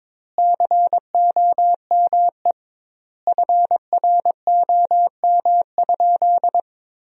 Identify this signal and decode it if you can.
Morse code, heard as a keyed tone, about 21 wpm, 700 Hz. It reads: COME FROM?
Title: 2x Word-Spacing - Sentences from Top 200 Words - 22wpm